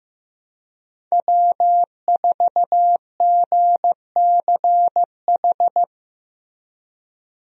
Morse code, heard as a keyed tone, about 16 wpm, 700 Hz. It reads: W4GCH
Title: USA Call Signs - 5 Characters - Repeated 3x - 15wpm